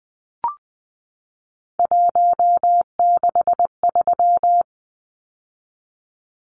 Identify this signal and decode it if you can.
Morse code, heard as a keyed tone, about 20 wpm, 700 Hz. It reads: 163
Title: Numbers - 3-Digit 20wpm